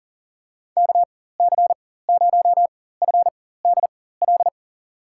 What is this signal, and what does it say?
Morse code, heard as a keyed tone, about 40 wpm, spaced as for 19 wpm, 700 Hz. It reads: KC0FDL
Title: USA Call Signs - 6 Characters - 4x Character Spacing - 40wpm